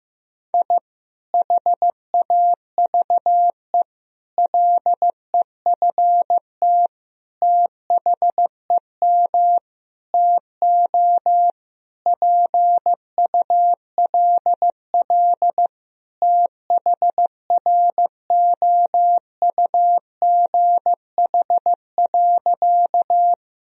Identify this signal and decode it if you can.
Morse code, heard as a keyed tone, about 16 wpm, 700 Hz. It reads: I HAVE LEFT THEM TO PULL THROUGH.
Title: Sentences from Top 1000 Words - Review Part 2 - 15wpm